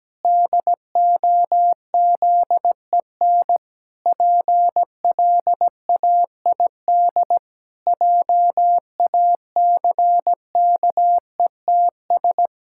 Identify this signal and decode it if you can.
Morse code, heard as a keyed tone, about 17 wpm, 700 Hz. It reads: DOZEN PLAID JACKETS
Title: Pangram Challenge v2 - 17wpm